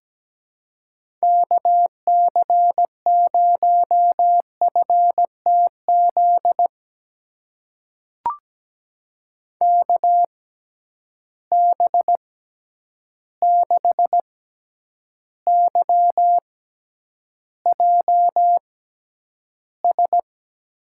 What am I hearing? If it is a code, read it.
Morse code, heard as a keyed tone, about 17 wpm, spaced as for 9 wpm, 700 Hz. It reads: KC0FTZ KB6YJS
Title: USA Call Signs - 6 Characters - 6x Character Spacing - 17wpm